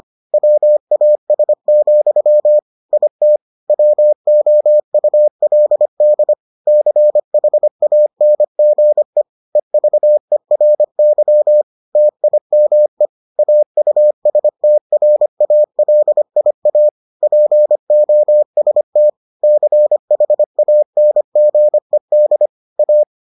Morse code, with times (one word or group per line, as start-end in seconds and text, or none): 0.3-2.6 WAS,
2.9-3.4 IT
3.7-6.3 WOULD
6.7-9.2 CHANGE
9.6-11.6 EVERY
12.0-13.1 TIME
13.4-16.9 AUSTRALIA
17.2-19.1 POST
19.4-22.5 CHANGED
22.8-23.0 A